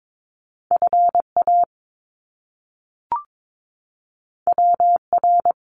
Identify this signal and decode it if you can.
Morse code, heard as a keyed tone, about 22 wpm, 700 Hz. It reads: FA WR
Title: Top Two Letter Combinations in Words 22wpm